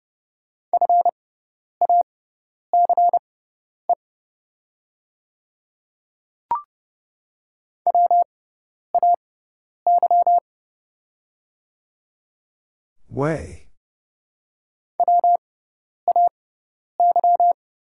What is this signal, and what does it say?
Morse code, heard as a keyed tone, about 30 wpm, spaced as for 10 wpm, 700 Hz. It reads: FACE WAY WAY
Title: 6x Character Spacing - Top 500 Words - 30wpm